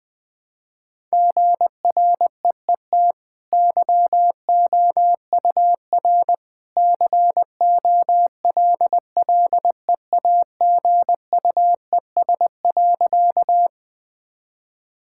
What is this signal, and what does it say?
Morse code, heard as a keyed tone, about 20 wpm, 700 Hz. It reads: GREET YOUR COLLEAGUES.
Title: Easy Sentences with Hints - 20wpm